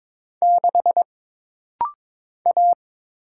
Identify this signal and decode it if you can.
Morse code, heard as a keyed tone, about 22 wpm, 700 Hz. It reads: 6 A
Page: This rapid-fire timing is geared to push your ICR (Instant Character Recognition) skills.